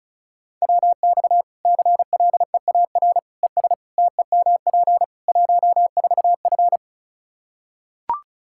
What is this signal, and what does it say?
Morse code, heard as a keyed tone, about 36 wpm, 700 Hz. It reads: WX CLEAR ES TEMP 14F
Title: K1USN SST QSO Practice 35wpm